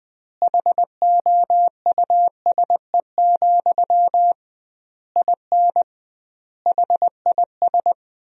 Morse code, heard as a keyed tone, about 20 wpm, 700 Hz. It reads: HOUSE, IN HIS
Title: 2x Word-Spacing - Sentences from Top 300 Words - 20wpm